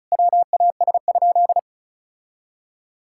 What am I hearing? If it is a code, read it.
Morse code, heard as a keyed tone, about 36 wpm, 700 Hz. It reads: WAS?
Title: Sentences from Top 1000 Words - Review Part 6 - 35wpm